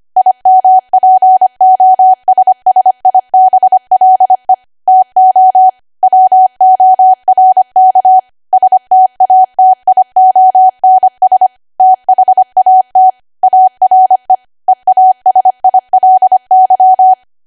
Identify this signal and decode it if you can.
Morse code, heard as a keyed tone, about 25 wpm, 750 Hz. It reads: IMPOSSIBLE TO WORK STATIONS THAT ARE EASILY
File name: Archive / 25 WPM / 25WPM